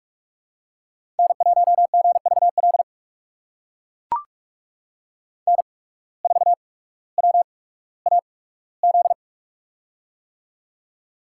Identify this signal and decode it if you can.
Morse code, heard as a keyed tone, about 44 wpm, spaced as for 19 wpm, 690 Hz. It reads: N1GVL N4WAZ
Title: USA Call Signs - 5 Characters - 8x Character Spacing - 45wpm